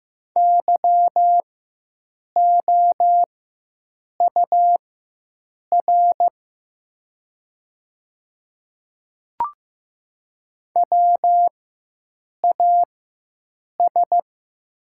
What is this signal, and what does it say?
Morse code, heard as a keyed tone, about 15 wpm, spaced as for 7 wpm, 700 Hz. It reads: YOUR WAS
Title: 4x Character Spacing - Top 100 Words - 15wpm